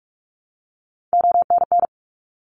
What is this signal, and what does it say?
Morse code, heard as a keyed tone, about 46 wpm, 700 Hz. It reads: ONN
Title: ARRL Sections 45wpm